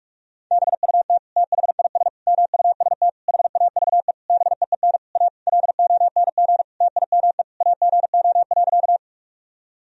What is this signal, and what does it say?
Morse code, heard as a keyed tone, about 41 wpm, 700 Hz. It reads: BUT THIS MUST HAVE BEEN A LONG TIME AGO.
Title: Sentences from Top 1000 Words - Review Part 3 - 45wpm